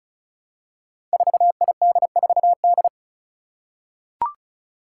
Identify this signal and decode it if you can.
Morse code, heard as a keyed tone, about 36 wpm, 700 Hz. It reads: 4ID4D
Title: Five Challenging Letter-Number Combinations 35wpm